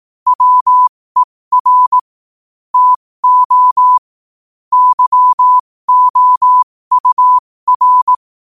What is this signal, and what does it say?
Morse code, heard as a keyed tone, about 18 wpm, spaced as for 15 wpm, 1000 Hz. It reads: WER TO YOUR
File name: qotdfiles / 15wpm